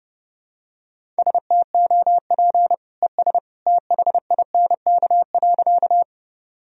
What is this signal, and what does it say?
Morse code, heard as a keyed tone, about 32 wpm, 700 Hz. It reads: STOP ES THINK.
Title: Sentences from Top 300 Words and Abbreviations 30wpm